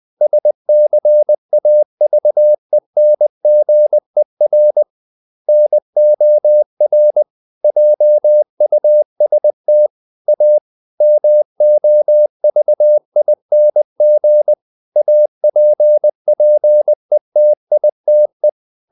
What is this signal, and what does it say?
Morse code, heard as a keyed tone, about 20 wpm, 600 Hz. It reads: SCAVENGER NOR JUST A MOVING APPETITE
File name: the old man and the sea part 2 20wpm